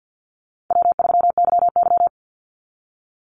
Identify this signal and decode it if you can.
Morse code, heard as a keyed tone, about 50 wpm, 700 Hz. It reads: W3YY
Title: Easy Call Signs Repeated 3x - 50wpm